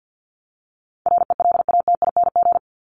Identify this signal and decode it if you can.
Morse code, heard as a keyed tone, about 49 wpm, 700 Hz. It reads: RELATING
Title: ING Suffix - Top 100 Words with Base Words 50wpm